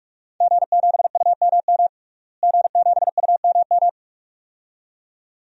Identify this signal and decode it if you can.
Morse code, heard as a keyed tone, about 48 wpm, 700 Hz. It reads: G7UMM G7UMM
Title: DXCC-100 Call Signs - 5 Characters - Repeated 2x - 45wpm